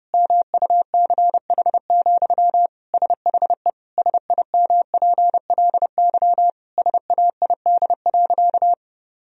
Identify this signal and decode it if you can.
Morse code, heard as a keyed tone, about 30 wpm, 700 Hz. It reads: MUCH, SHE SIMPLY SAID.
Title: Sentences from Top 1000 Words - Review Part 4 - 30wpm